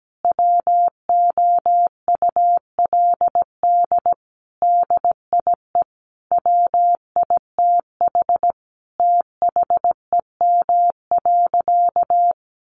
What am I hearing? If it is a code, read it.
Morse code, heard as a keyed tone, about 17 wpm, 700 Hz. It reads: WOULD DIE WITH THEM.
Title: Sentences from Top 1300 Words 17wpm